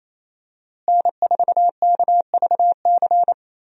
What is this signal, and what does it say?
Morse code, heard as a keyed tone, about 27 wpm, 700 Hz. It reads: N4KVC
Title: USA Call Signs - 5 Characters - Repeated 3x - 28wpm